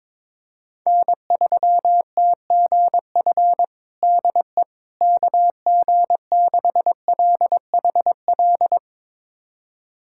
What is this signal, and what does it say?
Morse code, heard as a keyed tone, about 22 wpm, 700 Hz. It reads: N3TGF DE KG6LHL
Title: USA Callsign DE Callsign 22wpm